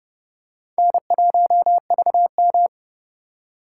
Morse code, heard as a keyed tone, about 30 wpm, 700 Hz. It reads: N1VM
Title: USA Call Signs - 4 Characters - 3x Character Spacing - 30wpm